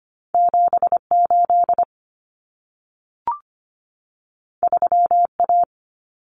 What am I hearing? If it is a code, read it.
Morse code, heard as a keyed tone, about 25 wpm, 700 Hz. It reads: 78 3A
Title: Common Cut Numbers - 2-Digit 25wpm